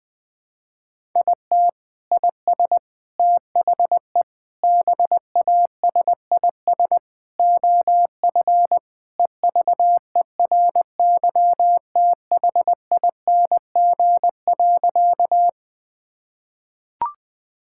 Morse code, heard as a keyed tone, about 21 wpm, 700 Hz. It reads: IT IS THE BASIS OF EVERYTHING.
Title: Sentences from Top 1000 Words - Review Part 5 - 20wpm